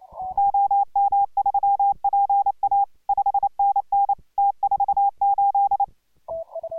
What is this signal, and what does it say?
Morse code, heard as a keyed tone, about 28 wpm, 810 Hz. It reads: OM3PA 5NN T48